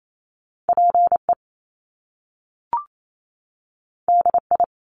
Morse code, heard as a keyed tone, about 28 wpm, 700 Hz. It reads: PE DI
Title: Top Two Letter Combinations in Words 28wpm